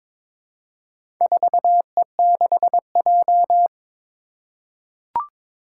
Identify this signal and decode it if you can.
Morse code, heard as a keyed tone, about 22 wpm, 700 Hz. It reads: 4E6J
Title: Four Challenging Letter-Number Combinations 22wpm